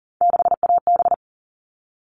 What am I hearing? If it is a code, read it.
Morse code, heard as a keyed tone, about 40 wpm, 700 Hz. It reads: 6AB
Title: All Cut Numbers - 3-Digit 40wpm